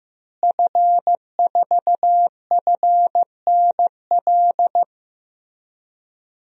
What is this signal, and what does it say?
Morse code, heard as a keyed tone, about 15 wpm, 700 Hz. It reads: F4FNL
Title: DXCC-100 Call Signs - 5 Characters - Repeated 3x - 15wpm